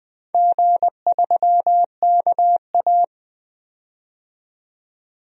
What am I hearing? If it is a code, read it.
Morse code, heard as a keyed tone, about 20 wpm, 700 Hz. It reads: G3KA
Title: DXCC-100 Call Signs - 4 Characters - 20wpm